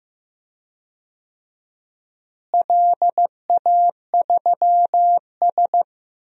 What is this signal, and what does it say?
Morse code, heard as a keyed tone, about 15 wpm, 700 Hz. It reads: LA3S
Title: DXCC-100 Call Signs - 4 Characters - Repeated 2x - 15wpm